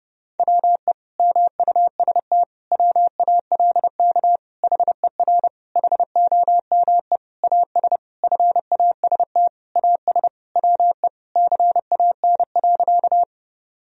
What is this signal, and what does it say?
Morse code, heard as a keyed tone, about 32 wpm, 700 Hz. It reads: WE MUST WALK HER HOME AS FAST AS WE CAN.
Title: Sentences from Top 1000 Words - Review Part 2 - 30wpm